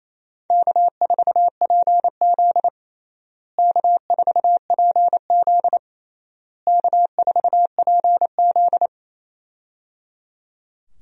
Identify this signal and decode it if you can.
Morse code, heard as a keyed tone, about 28 wpm, 700 Hz. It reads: K4PZ K4PZ K4PZ
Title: USA Call Signs - 4 Characters - Repeated 3x - 28wpm